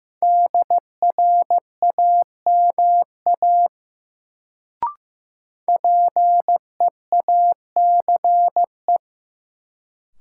Morse code, heard as a keyed tone, about 15 wpm, 700 Hz. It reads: DRAMA PEACE
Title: Sentences from Top 1500 Words - New Word Drill - 15wpm